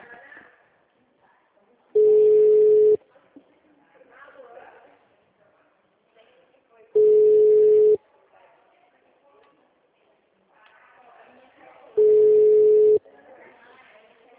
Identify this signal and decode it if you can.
Ringback tone (Europe, 425 Hz), with speech.
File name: out- -2006- - - .17643